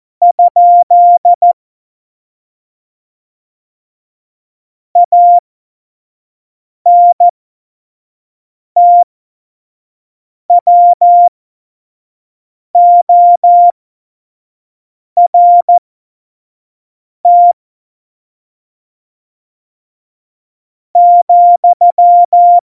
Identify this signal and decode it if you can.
Morse code, heard as a keyed tone, about 14 wpm, spaced as for 4 wpm, 700 Hz. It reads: ? ANTWORT,
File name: Morsekurs / Witze / Witz 25. Dezember 2018, 25 F70